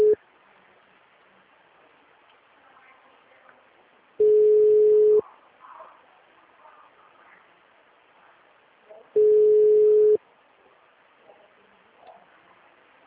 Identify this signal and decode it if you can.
Ringback tone (Europe, 425 Hz).
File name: out- -2010- - - .56